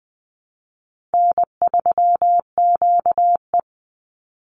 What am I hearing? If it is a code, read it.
Morse code, heard as a keyed tone, about 20 wpm, 700 Hz. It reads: N3QE
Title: Easy Call Signs Repeated 3x - 20wpm